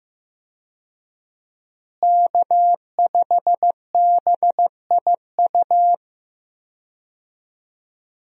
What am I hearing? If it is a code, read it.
Morse code, heard as a keyed tone, about 15 wpm, 700 Hz. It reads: K5BIU
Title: USA Call Signs - 5 Characters - Repeated 2x - 15wpm